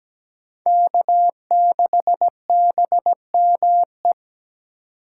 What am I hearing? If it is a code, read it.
Morse code, heard as a keyed tone, about 17 wpm, 700 Hz. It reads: K6BME
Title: USA Call Signs - 8x Character Spacing - 17wpm